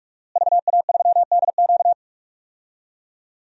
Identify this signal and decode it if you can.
Morse code, heard as a keyed tone, about 46 wpm, 700 Hz. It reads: VA3DQ
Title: DXCC-100 Call Signs - 5 Characters - Repeated 3x - 45wpm